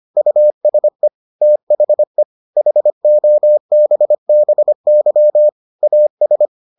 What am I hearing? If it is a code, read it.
Morse code, heard as a keyed tone, about 25 wpm, 600 Hz. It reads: USE THE HOBBY AS